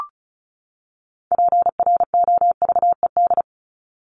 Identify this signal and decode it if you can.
Morse code, heard as a keyed tone, about 34 wpm, 700 Hz. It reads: PROVED
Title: ED Suffix - Top 100 Words with Base Words 35wpm